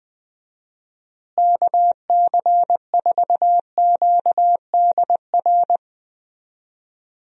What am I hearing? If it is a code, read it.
Morse code, heard as a keyed tone, about 21 wpm, 700 Hz. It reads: KC4QDR
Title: USA Call Signs - 6 Characters - Repeated 2x - 20wpm